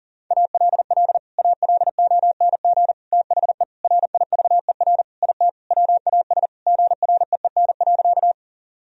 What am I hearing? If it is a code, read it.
Morse code, heard as a keyed tone, about 40 wpm, 700 Hz. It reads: ALL ALONG THE RIVER IT WAS GREEN.
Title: Sentences from Top 1000 Words - Review Part 2 - 40wpm